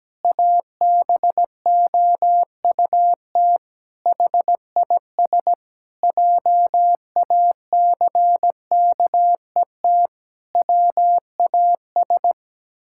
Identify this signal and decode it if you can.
Morse code, heard as a keyed tone, about 17 wpm, 700 Hz. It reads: ABOUT HIS JACKET WAS
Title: Pangram Challenge v2 - 17wpm